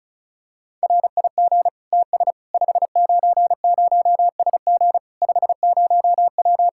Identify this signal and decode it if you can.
Morse code, heard as a keyed tone, about 36 wpm, 700 Hz. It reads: RIG TS 590SG 50W
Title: Practice QSO - Exchange 2 - Info - 35wpm